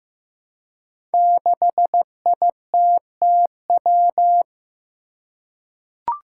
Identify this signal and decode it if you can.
Morse code, heard as a keyed tone, about 15 wpm, 700 Hz. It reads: 6ITTW
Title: Five Challenging Letter-Number Combinations 15wpm